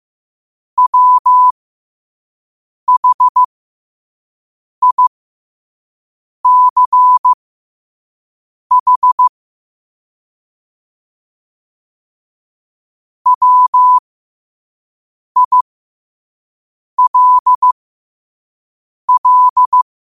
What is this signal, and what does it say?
Morse code, heard as a keyed tone, about 15 wpm, spaced as for 5 wpm, 1000 Hz. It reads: WHICH WILL
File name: qotdfiles / 5wpm